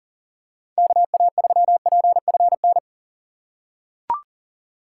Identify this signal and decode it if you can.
Morse code, heard as a keyed tone, about 40 wpm, 700 Hz. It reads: KA3PFN
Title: USA Call Signs - 6 Characters - 4x Character Spacing - 40wpm